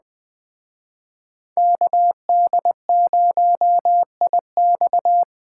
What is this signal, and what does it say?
Morse code, heard as a keyed tone, about 20 wpm, 700 Hz. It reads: KD0IX
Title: Speed Racing - USA 2x2 Call Signs 17wpm